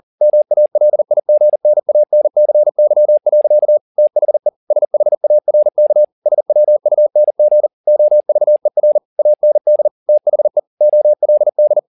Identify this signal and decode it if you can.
Morse code, heard as a keyed tone, about 40 wpm, 600 Hz. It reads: MALIGNANCY. THE SHARK SWUNG OVER AND THE OLD
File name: the old man and the sea 2 40wpm